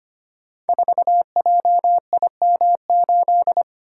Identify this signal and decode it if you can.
Morse code, heard as a keyed tone, about 25 wpm, 700 Hz. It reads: 4JIM8
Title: Code Groups - 5 Letter:Numbers - 25wpm